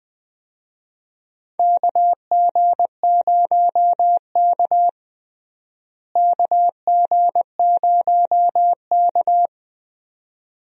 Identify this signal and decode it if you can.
Morse code, heard as a keyed tone, about 20 wpm, 700 Hz. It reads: KG0K KG0K
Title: USA Call Signs - 4 Characters - Repeated 2x - 20wpm